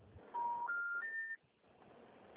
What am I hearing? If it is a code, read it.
Special information tone.
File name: out- -2011- - - .97